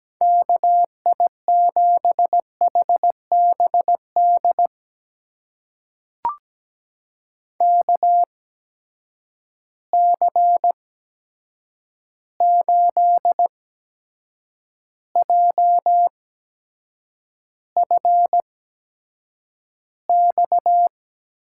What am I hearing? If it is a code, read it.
Morse code, heard as a keyed tone, about 17 wpm, spaced as for 7 wpm, 700 Hz. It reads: KI7HBD KC8JFX
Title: USA Call Signs - 6 Characters - 8x Character Spacing - 17wpm